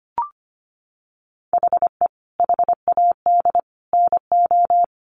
Morse code, heard as a keyed tone, about 25 wpm, 700 Hz. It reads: HE HAD NO
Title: Sentences from Top 1200 Words 25wpm